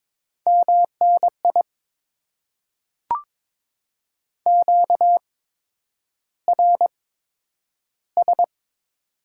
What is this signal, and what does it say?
Morse code, heard as a keyed tone, about 22 wpm, spaced as for 9 wpm, 700 Hz. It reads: MNI QRS
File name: Top 90 QSO Elements - 8x Character Spacing - 22wpm